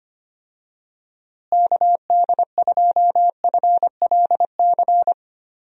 Morse code, heard as a keyed tone, about 25 wpm, 700 Hz. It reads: KD2FLC
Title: USA Call Signs - 6 Characters - Repeated 2x - 25wpm